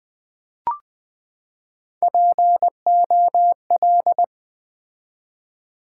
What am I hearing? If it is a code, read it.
Morse code, heard as a keyed tone, about 20 wpm, 700 Hz. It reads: POL
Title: Build A Word - 3 Letters At A Time - 9-Letter Words - 20wpm